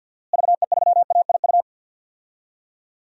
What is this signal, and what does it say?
Morse code, heard as a keyed tone, about 53 wpm, 700 Hz. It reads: VE3AIU
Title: DXCC-100 Call Signs - 6 Characters - Repeated 2x - 50wpm